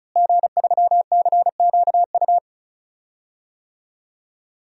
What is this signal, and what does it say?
Morse code, heard as a keyed tone, about 36 wpm, 700 Hz. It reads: G3CQU
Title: DXCC-100 Call Signs - 5 Characters - Repeated 2x - 35wpm